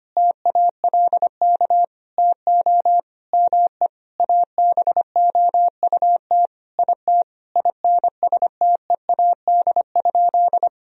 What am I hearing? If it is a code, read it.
Morse code, heard as a keyed tone, about 25 wpm, 700 Hz. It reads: TALK TO ME ABOUT IT INSTEAD?
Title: Sentences from Top 1000 Words - Review Part 4 - 25wpm